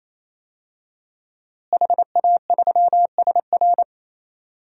Morse code, heard as a keyed tone, about 27 wpm, 700 Hz. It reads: HA3SR
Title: DXCC-100 Call Signs - 5 Characters - Repeated 3x - 28wpm